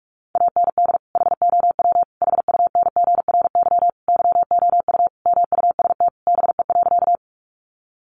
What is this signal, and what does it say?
Morse code, heard as a keyed tone, about 44 wpm, 700 Hz. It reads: AND HOW HUNGRY YOU MUST BE.
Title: Sentences from Top 1400 Words 45wpm